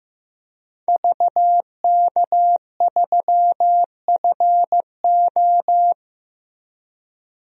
Morse code, heard as a keyed tone, about 15 wpm, 700 Hz. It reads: VK3FO
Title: DXCC-100 Call Signs - 5 Characters - 15wpm